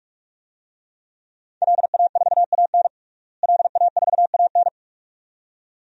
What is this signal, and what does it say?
Morse code, heard as a keyed tone, about 43 wpm, 700 Hz. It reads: LA4AN LA4AN
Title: DXCC-100 Call Signs - 5 Characters - Repeated 2x - 45wpm